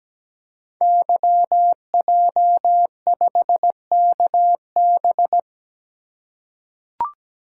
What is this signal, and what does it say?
Morse code, heard as a keyed tone, about 17 wpm, 700 Hz. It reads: YJ5KB
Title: Code Groups - 5 Letter:Numbers - 17wpm